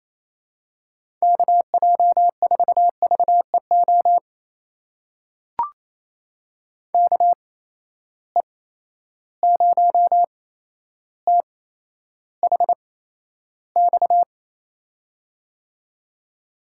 Morse code, heard as a keyed tone, about 27 wpm, spaced as for 12 wpm, 700 Hz. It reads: KJ4VEO KE0THX
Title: USA Call Signs - 6 Characters - 8x Character Spacing - 28wpm